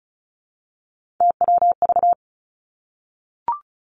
Morse code, heard as a keyed tone, about 36 wpm, 700 Hz. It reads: TWV
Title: Three Challenging Letter-Number Combinations 35wpm